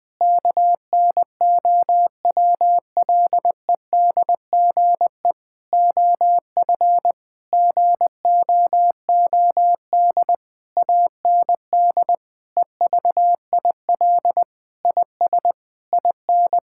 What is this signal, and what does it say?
Morse code, heard as a keyed tone, about 20 wpm, 700 Hz. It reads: KNOWLEDGE OF GOOD AND EVIL IS IN
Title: Sentences with 9-Word Phrases - 20wpm